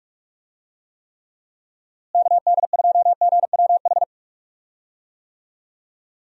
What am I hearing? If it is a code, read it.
Morse code, heard as a keyed tone, about 42 wpm, 700 Hz. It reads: KD2GWH
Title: USA Call Signs - 6 Characters - Repeated 3x - 45wpm